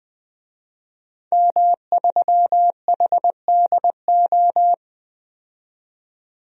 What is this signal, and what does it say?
Morse code, heard as a keyed tone, about 20 wpm, 700 Hz. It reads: M3HDO
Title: DXCC-100 Call Signs - 5 Characters - 20wpm